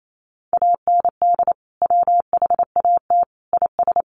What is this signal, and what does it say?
Morse code, heard as a keyed tone, about 28 wpm, 700 Hz. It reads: AND WHAT IS
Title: Sentences from Top 1500 Words 28wpm